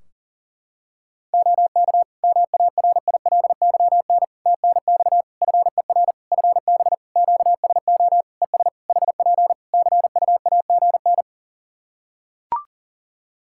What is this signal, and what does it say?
Morse code, heard as a keyed tone, about 37 wpm, 700 Hz. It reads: OK MARILYN TNX FER FB QSO ES HP CUAGN
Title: Practice QSO - Exchange 4 - Ending - 40wpm